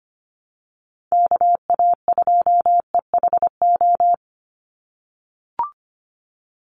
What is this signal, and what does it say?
Morse code, heard as a keyed tone, about 24 wpm, 700 Hz. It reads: KA2EHO
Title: USA Call Signs 25wpm